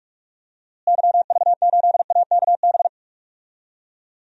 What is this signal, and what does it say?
Morse code, heard as a keyed tone, about 41 wpm, 700 Hz. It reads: YV8AKB
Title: DXCC-100 Call Signs - 6 Characters - Repeated 3x - 45wpm